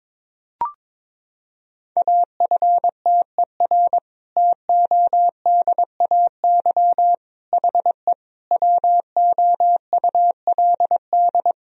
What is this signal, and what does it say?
Morse code, heard as a keyed tone, about 21 wpm, 700 Hz. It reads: AFTER TODAY HE WOULD
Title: Sentences from Top 1000 Words - Review Part 4 - 22wpm